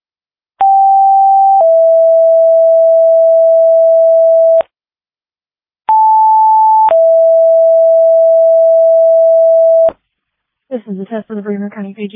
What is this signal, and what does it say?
Two-tone page, heard as a one-second tone, then a three-second tone, about 780 then 660 Hz, about 880 then 660 Hz, with speech. It is heard from 0.6 to 9.9 s.